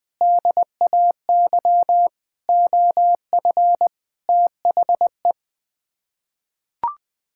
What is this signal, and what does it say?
Morse code, heard as a keyed tone, about 20 wpm, 700 Hz. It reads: DAY OF THE
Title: Sets of 6 Words - 20wpm